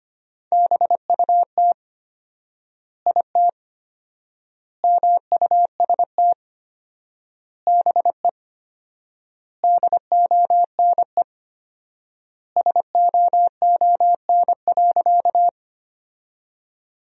Morse code, heard as a keyed tone, about 25 wpm, 700 Hz. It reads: BUT IT MUST BE DONE SOON.
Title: 4x Word-Spacing - Sentences from Top 500 Words - 25wpm